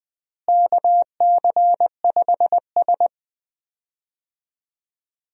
Morse code, heard as a keyed tone, about 20 wpm, 700 Hz. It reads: KC5S
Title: USA Call Signs - 4 Characters - Repeated 2x - 20wpm